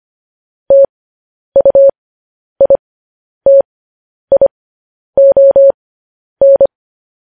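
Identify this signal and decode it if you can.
Morse code, heard as a keyed tone, about 25 wpm, spaced as for 10 wpm, 550 Hz. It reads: TUITION
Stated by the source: Morse code news headlines broadcast.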